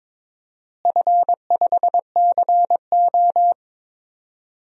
Morse code, heard as a keyed tone, about 22 wpm, 700 Hz. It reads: F5CO
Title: DXCC-100 Call Signs - 4 Characters - Repeated 2x - 22wpm